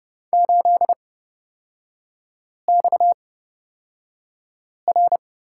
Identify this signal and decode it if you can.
Morse code, heard as a keyed tone, about 31 wpm, spaced as for 5 wpm, 690 Hz. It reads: 8XR
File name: Lesson 37e - Letter x - Three Characters 30wpm